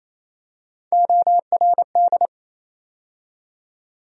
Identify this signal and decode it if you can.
Morse code, heard as a keyed tone, about 27 wpm, 700 Hz. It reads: ORD
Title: USA Call Sign Postfixes - 3 Letters 28wpm